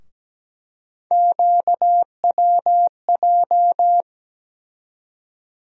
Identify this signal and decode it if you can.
Morse code, heard as a keyed tone, about 17 wpm, 700 Hz. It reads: QWJ
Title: USA Call Sign Postfixes - 3 Letters 17wpm